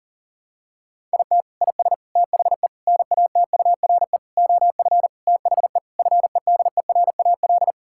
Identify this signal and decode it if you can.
Morse code, heard as a keyed tone, about 40 wpm, 700 Hz. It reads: IT IS THE NATURE OF THE FEDERAL
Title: Sentences from Top 1000 Words - Review Part 5 - 40wpm